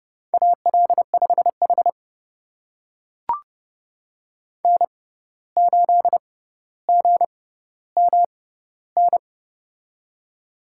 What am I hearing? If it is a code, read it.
Morse code, heard as a keyed tone, about 30 wpm, spaced as for 14 wpm, 700 Hz. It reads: AL5H N8GMN
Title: USA Call Signs - 6x Character Spacing - 30wpm